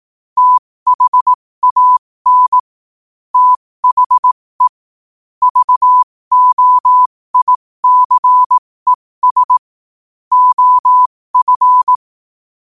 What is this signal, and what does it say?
Morse code, heard as a keyed tone, about 18 wpm, spaced as for 15 wpm, 1000 Hz. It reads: THAN THE VOICES OF
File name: qotdfiles / 15wpm